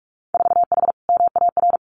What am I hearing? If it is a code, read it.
Morse code, heard as a keyed tone, about 44 wpm, 700 Hz. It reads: 4H MAR
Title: Winter Field Day Exchanges Encore 45wpm